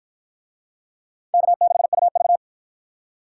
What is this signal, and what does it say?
Morse code, heard as a keyed tone, about 47 wpm, 700 Hz. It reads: K6UV